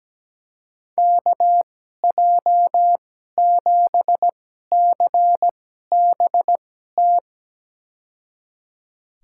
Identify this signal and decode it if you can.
Morse code, heard as a keyed tone, about 17 wpm, spaced as for 12 wpm, 700 Hz. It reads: KJ7CBT
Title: USA Call Signs - 6 Characters - 2x Character Spacing - 17wpm